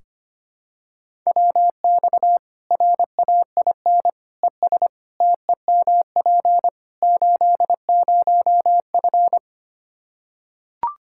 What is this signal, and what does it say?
Morse code, heard as a keyed tone, about 25 wpm, 700 Hz. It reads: WX RAIN ES TEMP 80F
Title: Practice QSO - Exchange 2 - Info - 25wpm